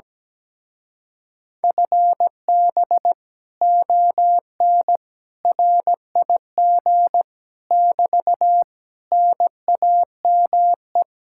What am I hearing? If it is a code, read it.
Morse code, heard as a keyed tone, about 17 wpm, 700 Hz. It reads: FB ON RIG <BT> NAME
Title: Speed-Racing Common QSO Phrases - 17wpm